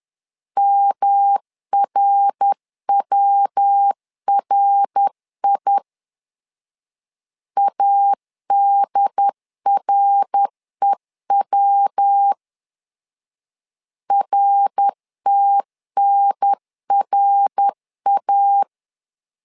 Morse code, heard as a keyed tone, about 10 wpm, 800 Hz. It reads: MRWRI ADREW RTNRA
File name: LEZIONI / 3 MWRO / 3 MWR 10wpm 1952